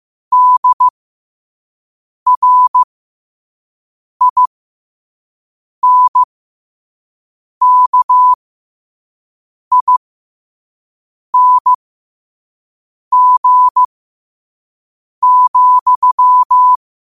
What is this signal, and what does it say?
Morse code, heard as a keyed tone, about 15 wpm, spaced as for 5 wpm, 1000 Hz. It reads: DRINKING,
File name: qotdfiles / 5wpm